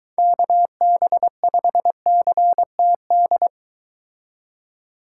Morse code, heard as a keyed tone, about 23 wpm, 700 Hz. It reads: KB5CTD